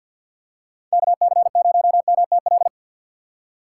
Morse code, heard as a keyed tone, about 47 wpm, 700 Hz. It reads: KC0MTL
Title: USA Call Signs - 2x Character Spacing - 50wpm